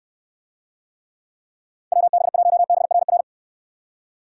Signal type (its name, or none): Morse code